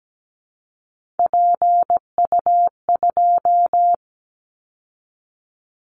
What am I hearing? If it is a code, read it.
Morse code, heard as a keyed tone, about 17 wpm, 700 Hz. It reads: PU2
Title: Callsign Prefixes for Top 100 DXCC - 17wpm